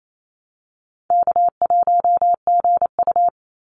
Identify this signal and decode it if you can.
Morse code, heard as a keyed tone, about 28 wpm, 700 Hz. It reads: K1GU
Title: Easy Call Signs Repeated 2x - 28wpm